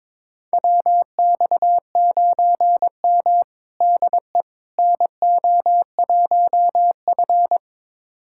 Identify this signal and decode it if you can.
Morse code, heard as a keyed tone, about 22 wpm, 700 Hz. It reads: WX9M DE NO1F
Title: USA Callsign DE Callsign - 4 Characters - 22wpm